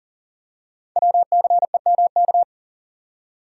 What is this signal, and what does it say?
Morse code, heard as a keyed tone, about 40 wpm, 690 Hz. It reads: WCEMK
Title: Code Groups - 5 Letters - 40wpm